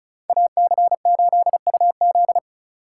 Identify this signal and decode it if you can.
Morse code, heard as a keyed tone, about 39 wpm, 700 Hz. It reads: AC8UZ